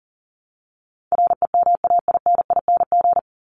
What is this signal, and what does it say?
Morse code, heard as a keyed tone, about 40 wpm, 700 Hz. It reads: REMAINING
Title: ING Suffix - Top 100 Words with Base Words 40wpm